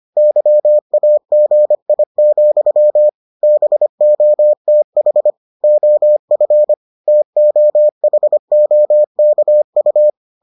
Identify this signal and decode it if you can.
Morse code, heard as a keyed tone, about 25 wpm, 600 Hz. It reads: YAGI, BOTH OF TOHOKU